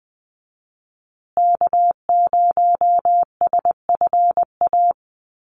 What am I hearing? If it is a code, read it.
Morse code, heard as a keyed tone, about 20 wpm, 700 Hz. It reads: K0SFA
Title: USA Call Signs Repeated 2x 20wpm